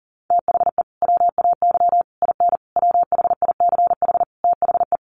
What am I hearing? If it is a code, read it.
Morse code, heard as a keyed tone, about 37 wpm, 700 Hz. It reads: THE WAY IN WHICH THE
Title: Sets of 5 Words Encore2 40wpm